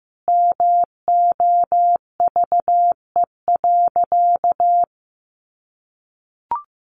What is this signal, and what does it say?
Morse code, heard as a keyed tone, about 15 wpm, 700 Hz. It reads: MOVE.
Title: Sentences from Top 1300 Words 15wpm